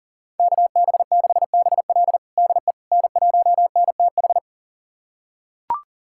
Morse code, heard as a keyed tone, about 37 wpm, 690 Hz. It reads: KB6BL DE N1NTH
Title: USA Callsign DE Callsign - 5 Characters - 40wpm